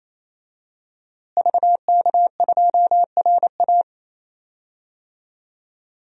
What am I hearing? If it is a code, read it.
Morse code, heard as a keyed tone, about 28 wpm, 700 Hz. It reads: VK2RA
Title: DXCC-100 Call Signs - 5 Characters - Repeated 3x - 28wpm